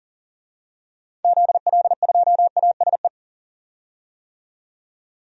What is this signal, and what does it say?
Morse code, heard as a keyed tone, about 40 wpm, 700 Hz. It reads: ZL2ASE
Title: DXCC-100 Call Signs - 6 Characters - 40wpm